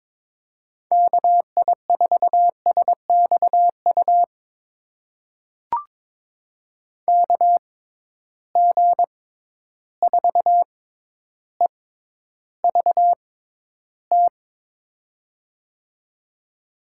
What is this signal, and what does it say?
Morse code, heard as a keyed tone, about 22 wpm, spaced as for 11 wpm, 700 Hz. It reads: KI4SXU KG4EVT
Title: USA Call Signs - 6x Character Spacing - 22wpm